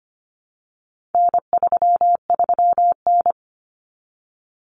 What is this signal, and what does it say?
Morse code, heard as a keyed tone, about 25 wpm, 700 Hz. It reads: N33N
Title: Common Cut Numbers - 4-Digit 25wpm